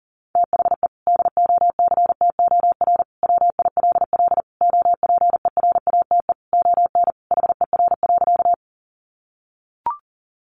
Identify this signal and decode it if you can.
Morse code, heard as a keyed tone, about 40 wpm, 700 Hz. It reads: THE DOCTOR WILL OPERATE ON HER.
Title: Sentences from Top 1500 Words 40wpm